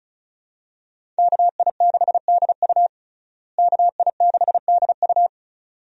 Morse code, heard as a keyed tone, about 34 wpm, 700 Hz. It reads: KI6DU KI6DU
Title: USA Call Signs - 5 Characters - Repeated 2x - 35wpm